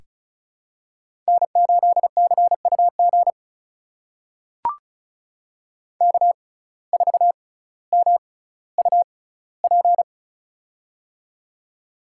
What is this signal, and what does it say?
Morse code, heard as a keyed tone, about 36 wpm, spaced as for 18 wpm, 700 Hz. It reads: N8CUG K4MUP
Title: USA Call Signs - 5 Characters - 6x Character Spacing - 35wpm